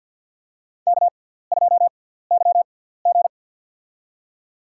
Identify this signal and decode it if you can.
Morse code, heard as a keyed tone, about 52 wpm, spaced as for 17 wpm, 700 Hz. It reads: K2YG
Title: USA Call Signs - 4 Characters - 6x Character Spacing - 50wpm